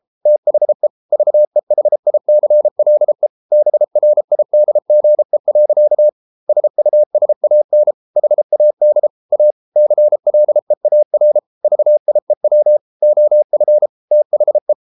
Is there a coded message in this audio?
Yes: Morse code from 0.3 to 14.7 s, heard as a keyed tone, about 33 wpm, 600 Hz. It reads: THE VEHICLE BRIDGE. SUSAN HAD A CLEAR VIEW OF THE